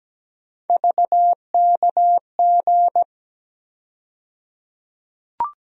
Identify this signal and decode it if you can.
Morse code, heard as a keyed tone, about 17 wpm, 700 Hz. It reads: VKG
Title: USA Call Sign Postfixes - 3 Letters 17wpm